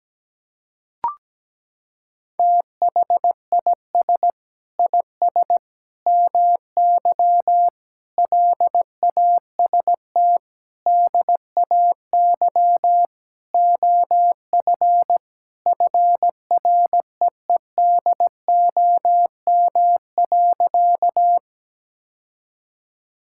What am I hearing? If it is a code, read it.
Morse code, heard as a keyed tone, about 17 wpm, 700 Hz. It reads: THIS IS MY LAST DAY OF FREEDOM.
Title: Sentences from Top 1000 Words - Review Part 5 - 17wpm